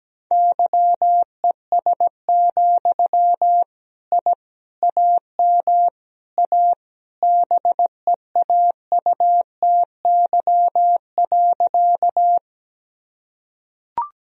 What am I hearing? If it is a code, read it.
Morse code, heard as a keyed tone, about 17 wpm, 700 Hz. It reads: YES, I AM A BEAUTY.
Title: Sentences from Top 1000 Words - Review Part 2 - 17wpm